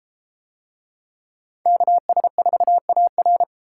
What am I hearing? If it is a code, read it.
Morse code, heard as a keyed tone, about 32 wpm, 700 Hz. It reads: KS4AR